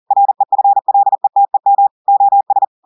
Morse code, heard as a keyed tone, about 40 wpm, 800 Hz. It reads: REFLETEM OS
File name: audios / portugais / portugais texte en clair 1 vitesse 40 mots minutes